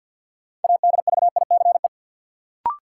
Morse code, heard as a keyed tone, about 52 wpm, 700 Hz. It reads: ADVICE
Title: Sentences from Top 1100 Words - New Word Drill Alt - 50wpm